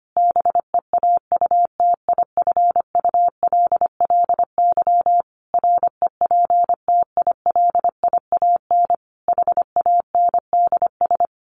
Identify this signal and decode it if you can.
Morse code, heard as a keyed tone, about 25 wpm, 700 Hz. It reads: BEAUTIFULLY REPTILIAN HANDS
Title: A Sound Of Thunder 25wpm